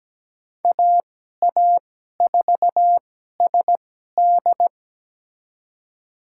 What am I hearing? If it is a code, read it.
Morse code, heard as a keyed tone, about 17 wpm, spaced as for 12 wpm, 700 Hz. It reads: AA4SD
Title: USA Call Signs - 5 Characters - 2x Character Spacing - 17wpm